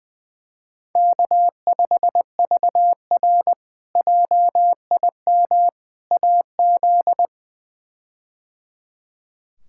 Morse code, heard as a keyed tone, about 20 wpm, 700 Hz. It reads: K5VR JIM AZ
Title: K1USN SST QSO Practice - 20wpm